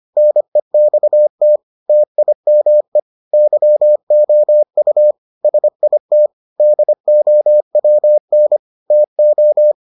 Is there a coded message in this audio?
Yes: Morse code from 0.2 to 9.7 s, heard as a keyed tone, about 25 wpm, 600 Hz. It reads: NEXT TIME YOU SIT DOWN TO